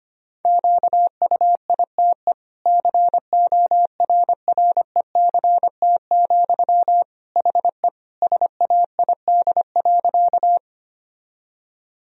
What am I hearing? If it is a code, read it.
Morse code, heard as a keyed tone, about 25 wpm, 700 Hz. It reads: QUITE CORRECT, HE SAID.
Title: Sentences from Top 1000 Words - Review Part 2 - 25wpm